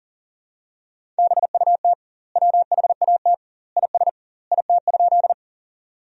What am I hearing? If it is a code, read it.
Morse code, heard as a keyed tone, about 37 wpm, 700 Hz. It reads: BUT WHAT IS IT?
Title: 2x Word-Spacing - Sentences from Top 100 Words - 40wpm